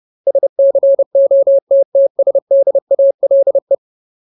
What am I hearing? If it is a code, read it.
Morse code, heard as a keyed tone, about 30 wpm, 550 Hz. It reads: SCOTTSDALE
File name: Ultimate Word List / 30WPM Ultimate